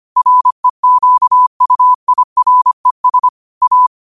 Morse code, heard as a keyed tone, about 25 wpm, 1000 Hz. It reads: REQUIRES A
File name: qotdfiles / 25wpm